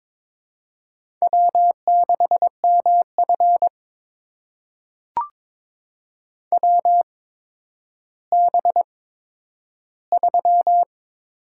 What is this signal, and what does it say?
Morse code, heard as a keyed tone, about 23 wpm, spaced as for 11 wpm, 700 Hz. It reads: W6MF WB3
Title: USA Call Signs - 8x Character Spacing - 22wpm